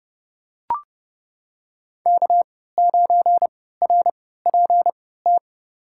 Morse code, heard as a keyed tone, about 31 wpm, spaced as for 17 wpm, 700 Hz. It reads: K9RPT
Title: USA Call Signs - 3x Character Spacing - 30wpm